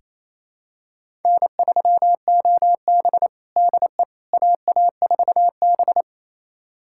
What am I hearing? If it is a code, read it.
Morse code, heard as a keyed tone, about 28 wpm, 700 Hz. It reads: N3OB DE AA4B
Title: USA Callsign DE Callsign - 4 Characters - 28wpm